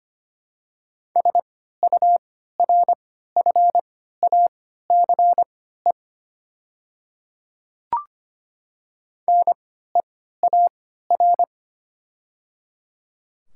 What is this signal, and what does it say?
Morse code, heard as a keyed tone, about 25 wpm, spaced as for 14 wpm, 690 Hz. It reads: SURFACE NEAR
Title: 3x Character Spacing - Top 500 Words - 25wpm